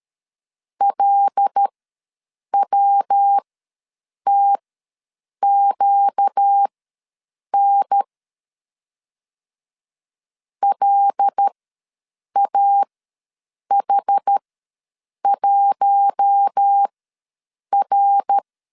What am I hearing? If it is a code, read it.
Morse code, heard as a keyed tone, about 13 wpm, spaced as for 7 wpm, 800 Hz. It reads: LWTQN LAH1R
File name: LEZIONI / 5 LBVC / 5 L ws 12wpm 2958